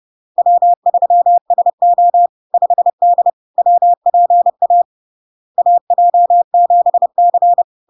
Morse code, heard as a keyed tone, about 30 wpm, 700 Hz. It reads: W3SO 5D WPA AJ7C
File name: LARK-FD-2022-Log-in-CW-30wpm-700-Hz